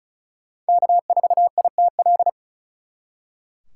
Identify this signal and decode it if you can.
Morse code, heard as a keyed tone, about 44 wpm, 700 Hz. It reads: K4ITL